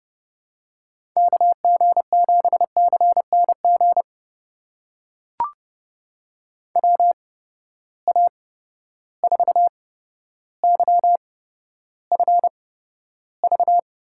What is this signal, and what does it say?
Morse code, heard as a keyed tone, about 30 wpm, spaced as for 13 wpm, 700 Hz. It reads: KG7CNG WA4YFV
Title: USA Call Signs - 6 Characters - 8x Character Spacing - 30wpm